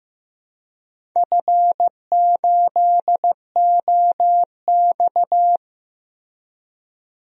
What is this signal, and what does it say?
Morse code, heard as a keyed tone, about 16 wpm, 700 Hz. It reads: F8OX
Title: DXCC-100 Call Signs - 4 Characters - Repeated 2x - 15wpm